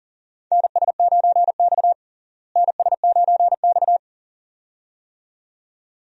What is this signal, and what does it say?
Morse code, heard as a keyed tone, about 40 wpm, 700 Hz. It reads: NS9X NS9X
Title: USA Call Signs - 4 Characters - Repeated 2x - 40wpm